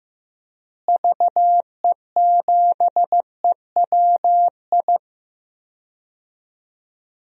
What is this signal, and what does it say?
Morse code, heard as a keyed tone, about 16 wpm, 700 Hz. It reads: VE7EWI
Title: DXCC-100 Call Signs - 6 Characters - Repeated 2x - 15wpm